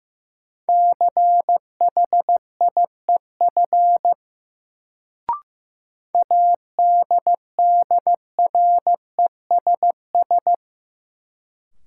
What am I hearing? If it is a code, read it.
Morse code, heard as a keyed tone, about 15 wpm, 700 Hz. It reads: CHIEF ADDRESS
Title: Sets of 5 Words Encore - New Word Drill - 15wpm